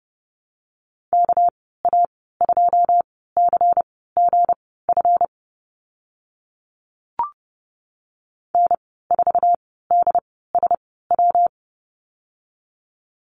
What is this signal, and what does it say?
Morse code, heard as a keyed tone, about 30 wpm, spaced as for 17 wpm, 700 Hz. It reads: KA2CGF N4DSW
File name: Lesson 37g - Letter x - Call Signs 30wpm